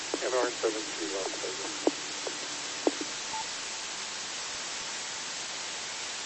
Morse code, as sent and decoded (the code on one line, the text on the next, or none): none